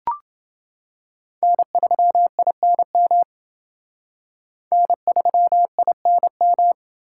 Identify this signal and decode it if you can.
Morse code, heard as a keyed tone, about 28 wpm, 700 Hz. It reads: N3INM N3INM